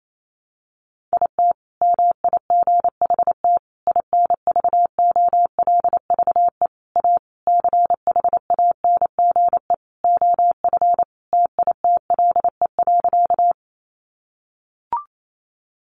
Morse code, heard as a keyed tone, about 28 wpm, 700 Hz. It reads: IT MIGHT INVOLVE A CHANGE OF TITLE.
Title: Sentences from Top 1200 Words 28wpm